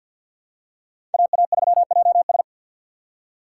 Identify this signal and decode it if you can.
Morse code, heard as a keyed tone, about 52 wpm, 700 Hz. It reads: AA3JS
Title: USA Call Signs - 5 Characters - 6x Character Spacing - 50wpm